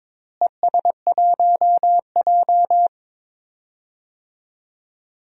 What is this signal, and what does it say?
Morse code, heard as a keyed tone, about 22 wpm, 700 Hz. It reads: ES1J
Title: DXCC-100 Call Signs - 4 Characters - Repeated 2x - 22wpm